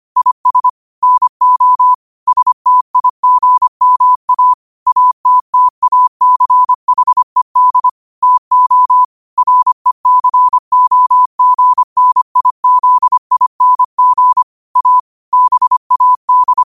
Morse code, heard as a keyed tone, about 25 wpm, 1000 Hz. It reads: IS NO STIGMA ATTACHED TO RECOGNIZING A BAD
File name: qotdfiles / 25wpm